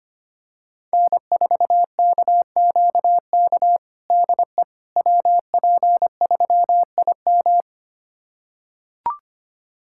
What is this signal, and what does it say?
Morse code, heard as a keyed tone, about 25 wpm, 700 Hz. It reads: N4KQK DE WP3IM
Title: USA Callsign DE Callsign - 5 Characters - 25wpm